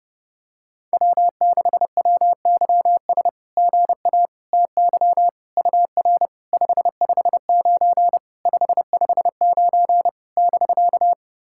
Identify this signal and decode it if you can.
Morse code, heard as a keyed tone, about 28 wpm, 690 Hz. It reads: W6WYS GA TY UR 559 559 <BK>
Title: SOTA QSO Practice 30wpm